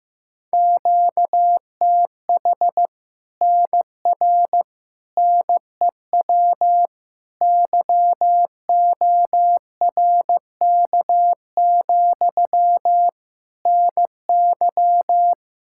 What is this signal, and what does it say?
Morse code, heard as a keyed tone, about 15 wpm, 700 Hz. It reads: QTH NR NEW YORK, NY
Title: Practice QSO - Exchange 1 - Beginning - 15wpm